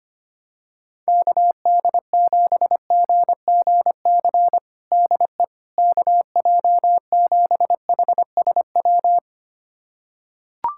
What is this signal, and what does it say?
Morse code, heard as a keyed tone, about 25 wpm, 700 Hz. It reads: KD7GGC DE KJ7HSW
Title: USA Callsign DE Callsign - 6 Characters - 25wpm